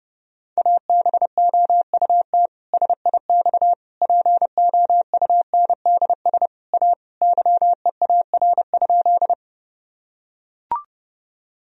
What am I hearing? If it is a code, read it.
Morse code, heard as a keyed tone, about 30 wpm, 700 Hz. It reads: ABOUT SIX POUNDS A YEAR?
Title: Sentences from Top 1000 Words - Review Part 2 - 30wpm